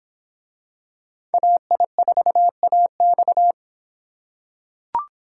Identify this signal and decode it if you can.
Morse code, heard as a keyed tone, about 25 wpm, 700 Hz. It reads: AI4AX